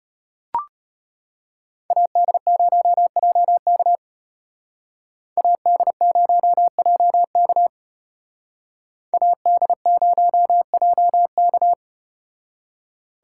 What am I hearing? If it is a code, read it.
Morse code, heard as a keyed tone, about 34 wpm, 700 Hz. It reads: AD0JK AD0JK AD0JK